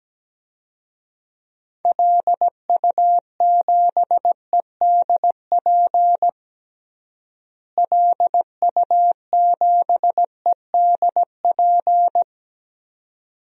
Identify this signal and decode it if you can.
Morse code, heard as a keyed tone, about 17 wpm, 700 Hz. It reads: LU7EDP LU7EDP
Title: DXCC-100 Call Signs - 6 Characters - Repeated 2x - 17wpm